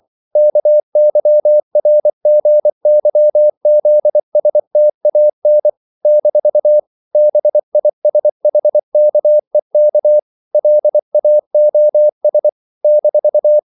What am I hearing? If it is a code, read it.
Morse code, heard as a keyed tone, about 24 wpm, 600 Hz. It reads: KYRGYZSTAN - BISHKEK LAOS -
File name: CW / Practice / NationalCapitals24